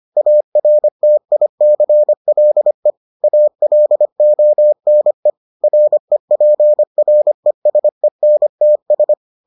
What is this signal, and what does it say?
Morse code, heard as a keyed tone, about 25 wpm, 600 Hz. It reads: ARTICLE ALONE REPRESENTS